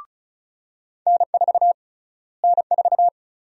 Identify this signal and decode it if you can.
Morse code, heard as a keyed tone, about 36 wpm, 700 Hz. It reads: N4 N4
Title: USA Call Sign Prefixes - Repeated 2x - 35wpm